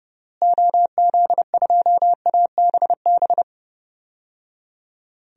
Morse code, heard as a keyed tone, about 30 wpm, 700 Hz. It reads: OZ2ABB
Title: DXCC-100 Call Signs - 6 Characters - Repeated 3x - 30wpm